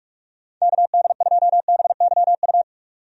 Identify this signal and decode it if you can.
Morse code, heard as a keyed tone, about 46 wpm, 700 Hz. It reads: KD2BYU
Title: USA Call Signs - 6 Characters - 45wpm